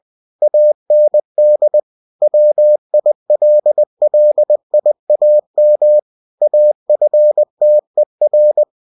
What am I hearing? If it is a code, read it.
Morse code, heard as a keyed tone, about 20 wpm, 600 Hz. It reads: AND WILLIAM AFTER
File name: cw-books / winnie-the-pooh / winnie-20wpm-page-0020